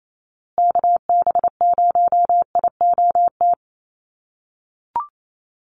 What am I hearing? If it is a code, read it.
Morse code, heard as a keyed tone, about 28 wpm, 700 Hz. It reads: KB0IOT
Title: USA Call Signs Repeated 2x 28wpm